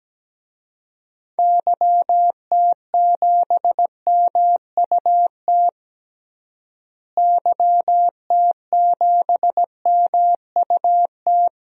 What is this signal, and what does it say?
Morse code, heard as a keyed tone, about 17 wpm, 700 Hz. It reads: YT7MUT YT7MUT
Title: DXCC-100 Call Signs - 6 Characters - Repeated 3x - 17wpm